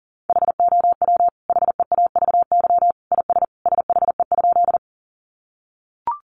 Morse code, heard as a keyed tone, about 40 wpm, 700 Hz. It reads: HOW HEAVY IS SHE?
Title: Sentences from Top 1100 Words 40wpm